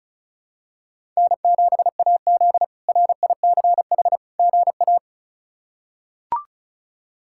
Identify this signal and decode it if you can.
Morse code, heard as a keyed tone, about 36 wpm, 700 Hz. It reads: N7AZ RICH GA
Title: K1USN SST QSO Practice - 35wpm